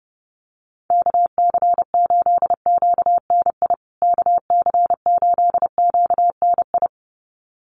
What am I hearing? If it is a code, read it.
Morse code, heard as a keyed tone, about 28 wpm, 700 Hz. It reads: KC8QNI KC8QNI
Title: USA Call Signs Repeated 2x 30wpm